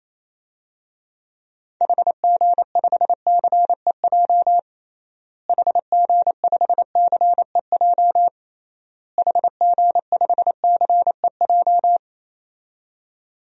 Morse code, heard as a keyed tone, about 28 wpm, 700 Hz. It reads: HG5CEJ HG5CEJ HG5CEJ
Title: DXCC-100 Call Signs - 6 Characters - Repeated 3x - 28wpm